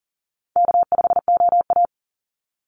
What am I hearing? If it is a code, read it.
Morse code, heard as a keyed tone, about 40 wpm, 700 Hz. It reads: K5OA
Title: Call Signs - 40wpm